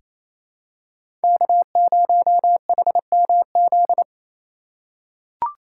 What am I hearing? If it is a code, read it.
Morse code, heard as a keyed tone, about 28 wpm, 700 Hz. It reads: K0HMZ
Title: K1USN SST QSO Practice - 28wpm